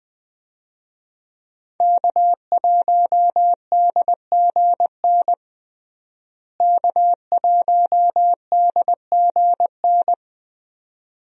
Morse code, heard as a keyed tone, about 20 wpm, 700 Hz. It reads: K1DGN K1DGN
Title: USA Call Signs - 5 Characters - Repeated 2x - 20wpm